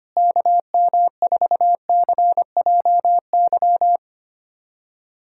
Morse code, heard as a keyed tone, about 25 wpm, 700 Hz. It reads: KM4CJY
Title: USA Callsign DE Callsign - 6 Characters - 25wpm